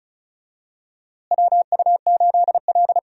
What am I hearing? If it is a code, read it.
Morse code, heard as a keyed tone, about 34 wpm, 700 Hz. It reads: WU8L
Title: USA Call Signs - 4 Characters - Repeated 3x - 35wpm